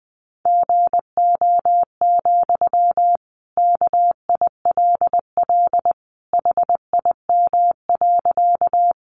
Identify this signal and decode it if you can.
Morse code, heard as a keyed tone, about 20 wpm, 700 Hz. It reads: GO, KILL HIM.
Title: Sentences from Top 1400 Words 20wpm